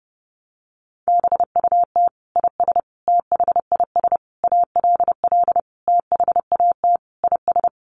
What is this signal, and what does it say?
Morse code, heard as a keyed tone, about 30 wpm, 700 Hz. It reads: BUT IS THIS ALL THAT IS
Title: Sentences from Top 600 Words 30wpm